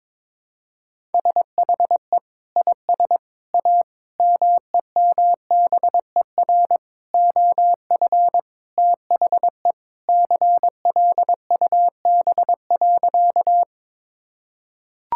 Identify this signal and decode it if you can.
Morse code, heard as a keyed tone, about 21 wpm, 700 Hz. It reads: SHE IS A MEMBER OF THE CLUB.
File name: practice / 22wpm / Sentences with 4-Word Phrases - 22wpm